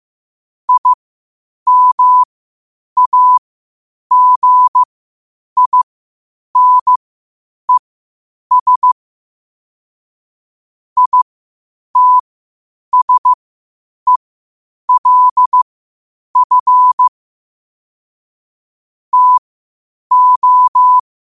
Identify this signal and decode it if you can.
Morse code, heard as a keyed tone, about 15 wpm, spaced as for 8 wpm, 1000 Hz. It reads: IMAGINES ITSELF TO
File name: qotdfiles / 8wpm